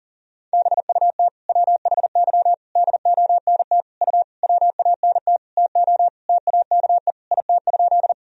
Morse code, heard as a keyed tone, about 40 wpm, 700 Hz. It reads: BUT WHY DONT U WANT TO TAKE IT?
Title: Sentences from Top 300 Words and Abbreviations 40wpm